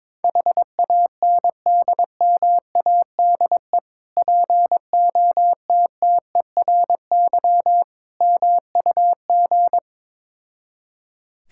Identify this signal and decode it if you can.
Morse code, heard as a keyed tone, about 22 wpm, 690 Hz. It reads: HANDMADE POTTERY MUG
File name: Difficult Words with Hints - 22wpm